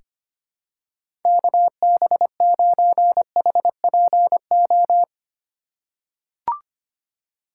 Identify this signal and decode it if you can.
Morse code, heard as a keyed tone, about 25 wpm, 700 Hz. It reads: KB9HPO
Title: USA Call Signs - 6 Characters - 4x Character Spacing - 25wpm